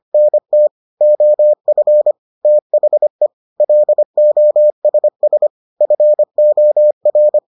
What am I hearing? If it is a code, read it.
Morse code, heard as a keyed tone, about 25 wpm, 600 Hz. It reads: NT OF THE LOSS FOR